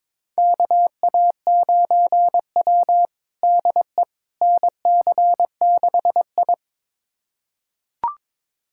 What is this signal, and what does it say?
Morse code, heard as a keyed tone, about 23 wpm, 700 Hz. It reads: KA9W DE NC6I
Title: USA Callsign DE Callsign - 4 Characters - 22wpm